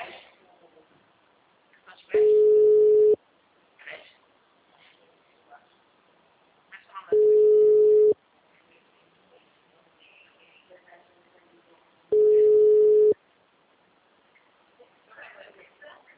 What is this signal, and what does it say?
Ringback tone (Europe, 425 Hz), with speech.